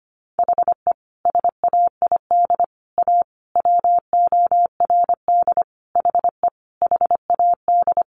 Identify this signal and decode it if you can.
Morse code, heard as a keyed tone, about 25 wpm, 700 Hz. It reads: HE SAID A WORD HE HAD
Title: Sentences from Top 600 Words 25wpm